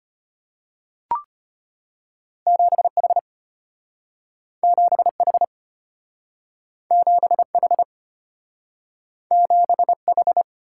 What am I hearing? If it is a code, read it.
Morse code, heard as a keyed tone, about 34 wpm, 700 Hz. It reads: 7H 7H 7H 7H